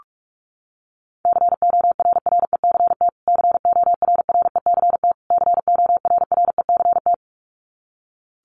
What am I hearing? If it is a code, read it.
Morse code, heard as a keyed tone, about 46 wpm, 700 Hz. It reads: CORRECT CORRECT CORRECT
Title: Top 500 Words Repeated 45wpm